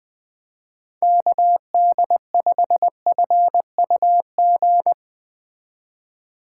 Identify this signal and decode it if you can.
Morse code, heard as a keyed tone, about 20 wpm, 700 Hz. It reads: KD5FUG
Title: Speed Racing - USA Call Signs 17wpm